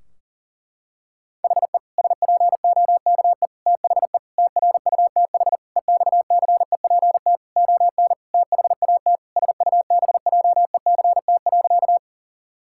Morse code, heard as a keyed tone, about 40 wpm, 700 Hz. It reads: HE SPOKE THE TRUTH EXCEPT ON THAT SUBJECT.
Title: Sentences from Top 1600 Words 40wpm